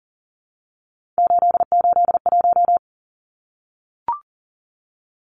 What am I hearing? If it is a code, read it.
Morse code, heard as a keyed tone, about 40 wpm, 700 Hz. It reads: 881
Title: Numbers - 3-Digit 40wpm